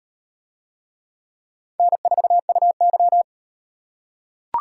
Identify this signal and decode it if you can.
Morse code, heard as a keyed tone, about 38 wpm, 700 Hz. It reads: N4UY